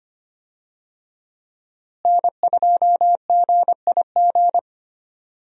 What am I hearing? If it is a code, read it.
Morse code, heard as a keyed tone, about 25 wpm, 700 Hz. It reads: N2GIG
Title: USA Call Signs - 5 Characters - Repeated 3x - 25wpm